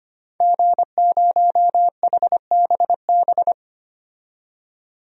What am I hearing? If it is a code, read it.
Morse code, heard as a keyed tone, about 25 wpm, 700 Hz. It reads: G0HBB
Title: DXCC-100 Call Signs - 5 Characters - 25wpm